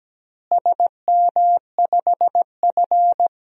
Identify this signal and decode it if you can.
Morse code, heard as a keyed tone, about 17 wpm, 700 Hz. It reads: SM5F
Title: DXCC-100 Call Signs - 4 Characters - Repeated 2x - 17wpm